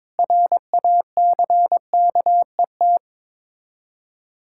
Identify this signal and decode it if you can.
Morse code, heard as a keyed tone, about 22 wpm, 700 Hz. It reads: RACKET
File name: Difficult Words with Hints - 22wpm